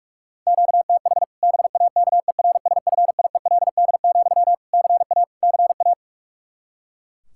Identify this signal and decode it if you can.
Morse code, heard as a keyed tone, about 46 wpm, 700 Hz. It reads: QTH BAKERSFIELD, CA CA
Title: Practice QSO - Exchange 1 - Beginning - 45wpm